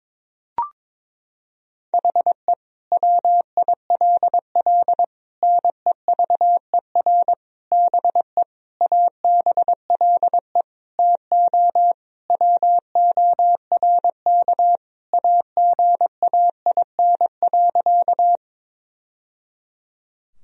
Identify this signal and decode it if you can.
Morse code, heard as a keyed tone, about 22 wpm, 700 Hz. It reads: HE WILL NEVER BE ABLE TO WORK AGAIN.
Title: Sentences from Top 1000 Words - Review Part 2 - 22wpm